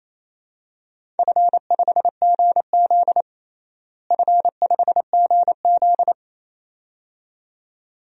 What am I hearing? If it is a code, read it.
Morse code, heard as a keyed tone, about 28 wpm, 700 Hz. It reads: F5GZ F5GZ
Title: DXCC-100 Call Signs - 4 Characters - Repeated 2x - 28wpm